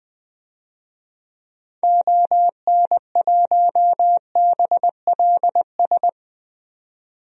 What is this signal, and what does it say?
Morse code, heard as a keyed tone, about 20 wpm, 700 Hz. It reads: ON1BLS
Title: DXCC-100 Call Signs - 6 Characters - 20wpm